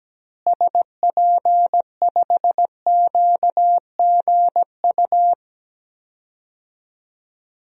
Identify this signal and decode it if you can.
Morse code, heard as a keyed tone, about 17 wpm, 700 Hz. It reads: SP5QGU
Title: DXCC-100 Call Signs - 6 Characters - Repeated 3x - 17wpm